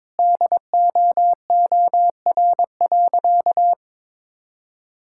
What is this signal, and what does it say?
Morse code, heard as a keyed tone, about 21 wpm, 700 Hz. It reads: DOOR.
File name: practice / 22wpm / Sentences with 2-Word Phrases - 22wpm